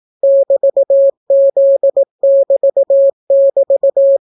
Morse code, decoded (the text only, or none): <BT>Z<BT><BT>